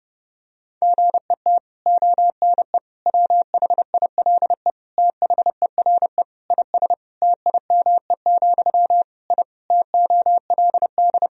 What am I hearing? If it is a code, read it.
Morse code, heard as a keyed tone, about 30 wpm, 700 Hz. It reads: GET ONE WHILE THERE IS TIME, I TOLD
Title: Sentences from Top 1000 Words - Review Part 1 - 30wpm